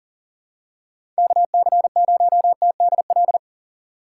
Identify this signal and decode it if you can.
Morse code, heard as a keyed tone, about 40 wpm, 700 Hz. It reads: KC0TDL
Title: USA Call Signs - 6 Characters - 4x Character Spacing - 40wpm